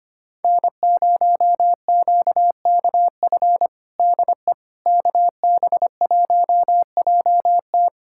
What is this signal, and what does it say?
Morse code, heard as a keyed tone, about 25 wpm, 700 Hz. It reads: N0QKF DE KB1JT
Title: USA Callsign DE Callsign - 5 Characters - 25wpm